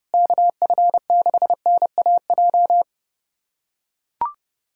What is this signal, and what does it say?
Morse code, heard as a keyed tone, about 30 wpm, 700 Hz. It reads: KF6NAJ
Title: USA Callsign DE Callsign - 6 Characters - 30wpm